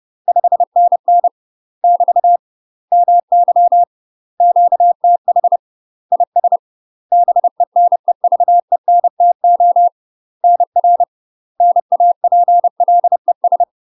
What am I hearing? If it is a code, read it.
Morse code, heard as a keyed tone, about 30 wpm, 700 Hz. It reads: 5NN <BT> MY QTH IS BENEVENTO NR NAPLES
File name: UK MASTER QSOs TRAINING / 30WPM / UK MASTER QSOs 0015